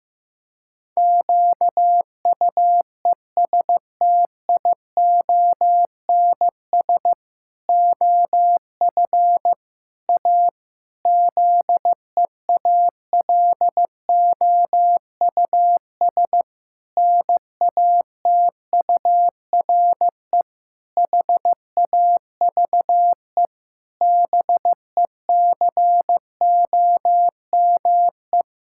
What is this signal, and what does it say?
Morse code, heard as a keyed tone, about 15 wpm, 700 Hz. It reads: QUESTIONS OF A ZEALOUS NATURE HAVE BECOME
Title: Pangram Challenge v2 - 15wpm